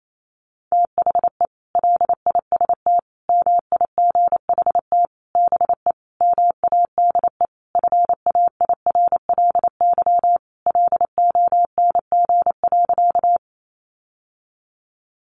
Morse code, heard as a keyed tone, about 28 wpm, 700 Hz. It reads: THE LIST MIGHT BE MADE FAIRLY LONG.
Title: Sentences from Top 1100 Words 28wpm